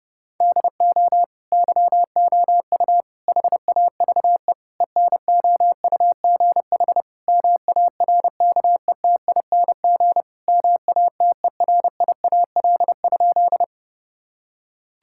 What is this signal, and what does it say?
Morse code, heard as a keyed tone, about 30 wpm, 700 Hz. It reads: DO YOU HAVE ENOUGH MARKETING MATERIAL?
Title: Sentences from Top 1000 Words - Review Part 5 - 30wpm